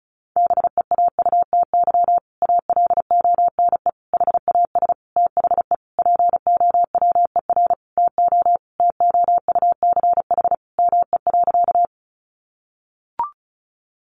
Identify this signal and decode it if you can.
Morse code, heard as a keyed tone, about 34 wpm, 700 Hz. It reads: BEAUTY ALONE HAS THE POWER TO TOUCH ME.
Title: Sentences from Top 1200 Words 35wpm